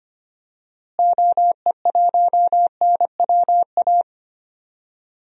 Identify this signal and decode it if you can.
Morse code, heard as a keyed tone, about 25 wpm, 700 Hz. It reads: OE1NWA
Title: DXCC-100 Call Signs - 6 Characters - Repeated 2x - 25wpm